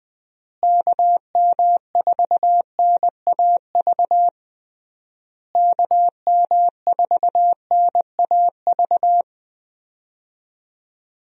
Morse code, heard as a keyed tone, about 20 wpm, 700 Hz. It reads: KM4NAV KM4NAV
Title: USA Call Signs - 6 Characters - Repeated 3x - 20wpm